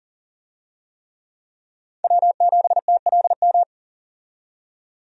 Morse code, heard as a keyed tone, about 40 wpm, 700 Hz. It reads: W7TLM
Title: USA Call Signs - 5 Characters - 40wpm